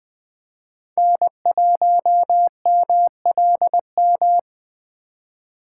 Morse code, heard as a keyed tone, about 20 wpm, 700 Hz. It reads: N1MLM
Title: USA Call Signs - 4x Character Spacing - 20wpm